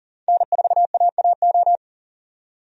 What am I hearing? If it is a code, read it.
Morse code, heard as a keyed tone, about 40 wpm, 700 Hz. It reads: N4AAO
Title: USA Call Signs - 5 Characters - 40wpm